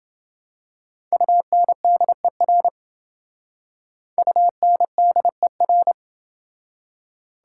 Morse code, heard as a keyed tone, about 29 wpm, 700 Hz. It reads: UNDER UNDER